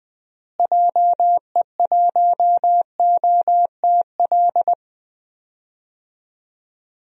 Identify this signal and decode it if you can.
Morse code, heard as a keyed tone, about 20 wpm, 690 Hz. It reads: JE1OTL
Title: DXCC-100 Call Signs - 6 Characters - Repeated 2x - 20wpm